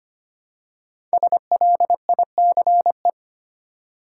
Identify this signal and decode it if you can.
Morse code, heard as a keyed tone, about 25 wpm, 700 Hz. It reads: SLICE
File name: 5 Letter Words - First 3 Letters Given - 25wpm